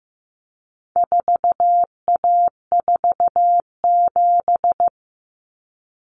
Morse code, heard as a keyed tone, about 15 wpm, 700 Hz. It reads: 4A47
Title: Common Cut Numbers - 4-Digit 15wpm